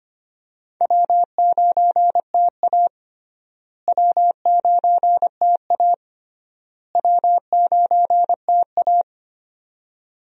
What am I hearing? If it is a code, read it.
Morse code, heard as a keyed tone, about 25 wpm, 700 Hz. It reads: W9TA W9TA W9TA
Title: USA Call Signs - 4 Characters - Repeated 3x - 25wpm